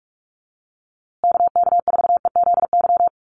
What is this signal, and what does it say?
Morse code, heard as a keyed tone, about 44 wpm, 700 Hz. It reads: KK4EZY
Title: USA Call Signs Repeated 2x 45wpm